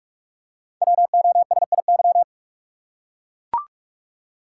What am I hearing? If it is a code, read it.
Morse code, heard as a keyed tone, about 44 wpm, 700 Hz. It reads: WOSIY
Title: Code Groups - 5 Letter:Numbers - 45wpm